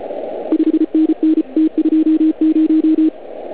Morse code, heard as a keyed tone, about 33 wpm, 320 Hz. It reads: 5NN T20